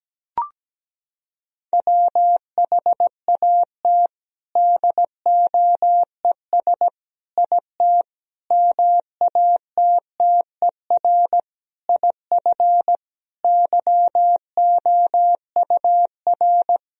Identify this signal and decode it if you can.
Morse code, heard as a keyed tone, about 17 wpm, 700 Hz. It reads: WHAT DOES IT MATTER IF YOUR
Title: Sentences from Top 1000 Words - Review Part 1 - 17wpm